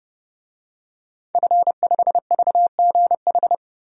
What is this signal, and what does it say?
Morse code, heard as a keyed tone, about 29 wpm, 700 Hz. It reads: F5VGH
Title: DXCC-100 Call Signs - 5 Characters - Repeated 3x - 30wpm